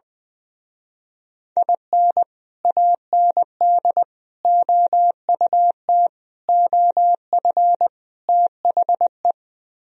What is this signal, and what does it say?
Morse code, heard as a keyed tone, about 21 wpm, 700 Hz. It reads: IN AND OUT OF THE
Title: Speed-Racing - Sets of 5 Words 20wpm